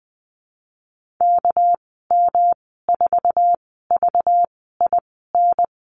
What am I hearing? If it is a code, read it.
Morse code, heard as a keyed tone, about 20 wpm, spaced as for 14 wpm, 700 Hz. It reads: KM4VIN
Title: 200 Lesson 32g - Letter k - Call Signs 20wpm